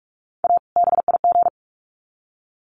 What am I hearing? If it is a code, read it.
Morse code, heard as a keyed tone, about 44 wpm, 700 Hz. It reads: A BIG
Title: Sets of 2 Words 45wpm